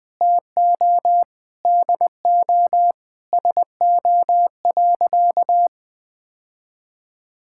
Morse code, heard as a keyed tone, about 20 wpm, 700 Hz. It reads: TO DO SO.
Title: Sentences from Top 1000 Words - Review Part 5 - 20wpm